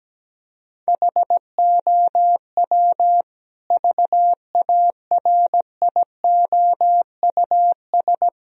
Morse code, heard as a keyed tone, about 17 wpm, 700 Hz. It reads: HOW VARIOUS
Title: Sentences from Top 1000 Words - Review Part 4 - 17wpm